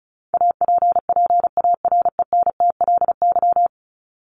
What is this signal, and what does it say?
Morse code, heard as a keyed tone, about 34 wpm, 700 Hz. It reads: APPARENTLY
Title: LY Suffix - Top 100 Words with Base Words 35wpm